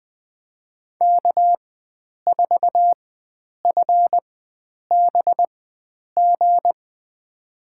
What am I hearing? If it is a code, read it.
Morse code, heard as a keyed tone, about 20 wpm, spaced as for 9 wpm, 700 Hz. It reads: K4FBG
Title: USA Call Signs - 5 Characters - 4x Character Spacing - 20wpm